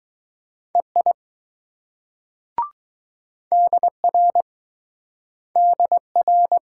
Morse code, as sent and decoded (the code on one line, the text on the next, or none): . .. / -.. .-. / -.. .-.
EI DR DR